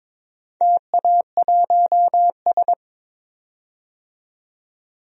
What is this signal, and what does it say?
Morse code, heard as a keyed tone, about 22 wpm, 700 Hz. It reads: TA1S
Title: DXCC-100 Call Signs - 4 Characters - 22wpm